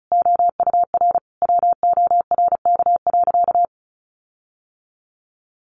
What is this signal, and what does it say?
Morse code, heard as a keyed tone, about 34 wpm, 700 Hz. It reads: OUR WORK.
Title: Sentences from Top 1300 Words 35wpm